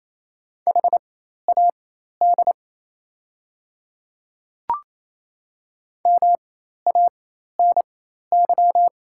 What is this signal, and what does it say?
Morse code, heard as a keyed tone, about 28 wpm, spaced as for 13 wpm, 700 Hz. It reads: HAD MANY
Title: 4x Character Spacing - Top 100 Words - 28wpm